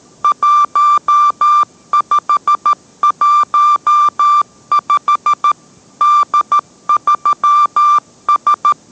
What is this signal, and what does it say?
Morse code, heard as a keyed tone, about 13 wpm, 1180 Hz. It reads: 1515 D3S